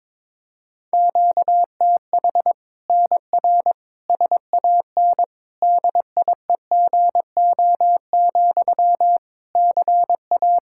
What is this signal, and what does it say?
Morse code, heard as a keyed tone, about 22 wpm, 700 Hz. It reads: QTH NR SAN DIEGO, CA
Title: Practice QSO - Exchange 1 - Beginning - 22wpm